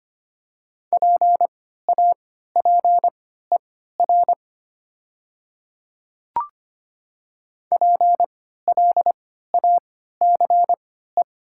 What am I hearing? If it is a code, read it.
Morse code, heard as a keyed tone, about 25 wpm, spaced as for 14 wpm, 700 Hz. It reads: PAPER PLACE
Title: 3x Character Spacing - Top 300 Words - 25wpm